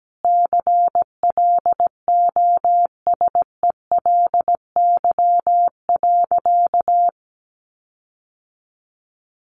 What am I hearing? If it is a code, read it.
Morse code, heard as a keyed tone, about 17 wpm, 700 Hz. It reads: CLOSELY.
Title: Sentences from Top 1400 Words 17wpm